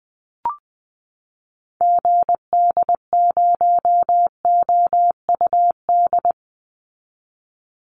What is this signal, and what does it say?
Morse code, heard as a keyed tone, about 20 wpm, 700 Hz. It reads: GD0OUD
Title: Call Signs Repeated 3x - 20wpm